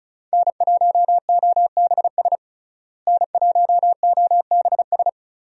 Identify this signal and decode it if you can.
Morse code, heard as a keyed tone, about 34 wpm, 700 Hz. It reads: N1OBS N1OBS
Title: USA Call Signs - 5 Characters - Repeated 3x - 35wpm